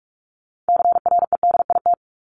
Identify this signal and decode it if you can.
Morse code, heard as a keyed tone, about 44 wpm, 700 Hz. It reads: CREDIT
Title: Top 751-1000 Words 45wpm